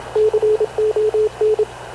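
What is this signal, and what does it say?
Morse code, heard as a keyed tone, about 27 wpm, 430 Hz, with radio noise. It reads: CON
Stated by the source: Havana Radio / CLA sending Spanish press (PX) in Morse, recorded by Radio Officer Paul Dumesnil onboard CNAV QUEST/CZDO.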